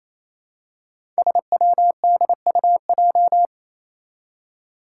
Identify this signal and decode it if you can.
Morse code, heard as a keyed tone, about 28 wpm, 700 Hz. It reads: SWDUJ
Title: Five Challenging Letter-Number Combinations 28wpm